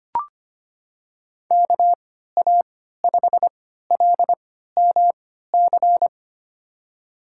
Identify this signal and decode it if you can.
Morse code, heard as a keyed tone, about 25 wpm, spaced as for 14 wpm, 700 Hz. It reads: KA5LMC
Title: USA Call Signs - 6 Characters - 3x Character Spacing - 25wpm